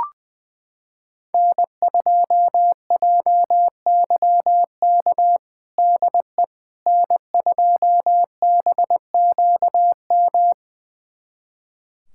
Morse code, heard as a keyed tone, about 21 wpm, 700 Hz. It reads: N2JYK DE N2BQM
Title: USA Callsign DE Callsign - 5 Characters - 20wpm